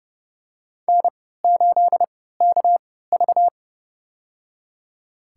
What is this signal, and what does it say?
Morse code, heard as a keyed tone, about 30 wpm, spaced as for 17 wpm, 700 Hz. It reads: N8KV
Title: USA Call Signs - 4 Characters - 3x Character Spacing - 30wpm